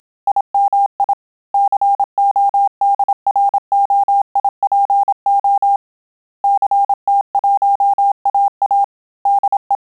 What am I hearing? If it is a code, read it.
Morse code, heard as a keyed tone, about 27 wpm, 800 Hz. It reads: IMI CODROIPO CT1AA DE
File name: Telegraphy-group / QSO CW 25 PPM